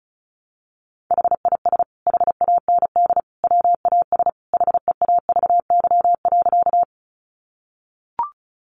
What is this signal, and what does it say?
Morse code, heard as a keyed tone, about 34 wpm, 700 Hz. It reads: HIS HAND WAS HEAVY.
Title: Sentences from Top 1100 Words 35wpm